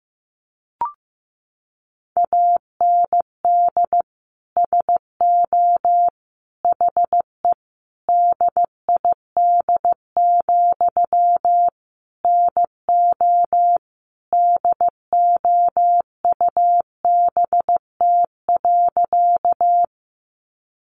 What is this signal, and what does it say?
Morse code, heard as a keyed tone, about 15 wpm, 700 Hz. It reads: AND SO HE DID, NO DOUBT.
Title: Sentences from Top 1500 Words 15wpm